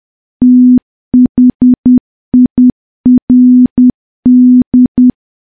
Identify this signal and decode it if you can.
Morse code, heard as a keyed tone, about 10 wpm, 250 Hz. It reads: THIRD
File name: morse / morse-training-10WPM